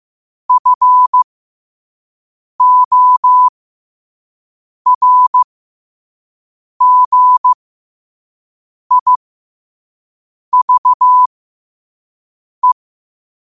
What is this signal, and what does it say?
Morse code, heard as a keyed tone, about 15 wpm, spaced as for 5 wpm, 1000 Hz. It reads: FORGIVE
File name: qotdfiles / 5wpm